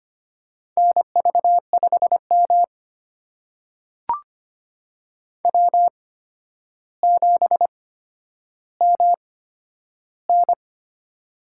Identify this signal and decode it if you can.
Morse code, heard as a keyed tone, about 25 wpm, spaced as for 11 wpm, 700 Hz. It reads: NV5M W7MN
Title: USA Call Signs - 4 Characters - 8x Character Spacing - 25wpm